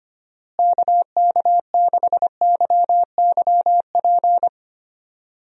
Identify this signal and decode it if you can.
Morse code, heard as a keyed tone, about 25 wpm, 700 Hz. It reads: KK6YYP
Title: USA Call Signs - 6 Characters - Repeated 2x - 25wpm